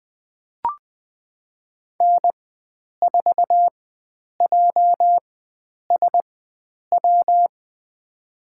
Morse code, heard as a keyed tone, about 21 wpm, spaced as for 9 wpm, 700 Hz. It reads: N4JSW
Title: USA Call Signs - 5 Characters - 4x Character Spacing - 20wpm